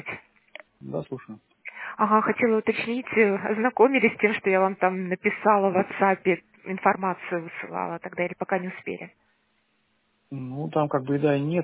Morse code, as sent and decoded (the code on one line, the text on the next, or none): none